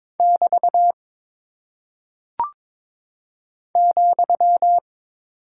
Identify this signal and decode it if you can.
Morse code, heard as a keyed tone, about 22 wpm, 700 Hz. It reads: <BT> ,
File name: practice / 22wpm / Single Punctuation-Prosign - 22wpm